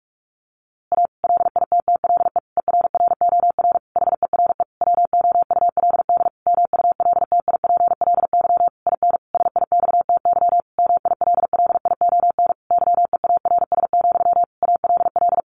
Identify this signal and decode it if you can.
Morse code, heard as a keyed tone, about 44 wpm, 700 Hz. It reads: A LITTLE ERROR HERE WOULD MULTIPLY IN SIXTY MILLION YEARS, ALL
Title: A Sound Of Thunder 45wpm